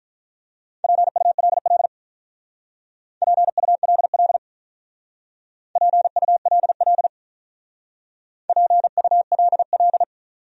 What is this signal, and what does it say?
Morse code, heard as a keyed tone, about 43 wpm, 700 Hz. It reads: PULL PULL PULL PULL
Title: Speed-Racing Top 500 Words 35wpm